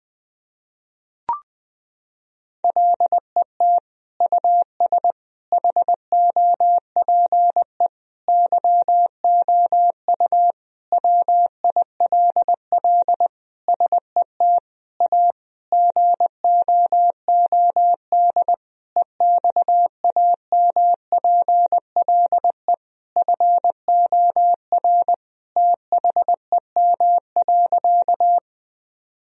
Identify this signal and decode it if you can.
Morse code, heard as a keyed tone, about 20 wpm, 700 Hz. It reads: LET US HOPE YOU WILL SET A GOOD EXAMPLE FOR THEM.
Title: Sentences from Top 1000 Words - Review Part 1 - 20wpm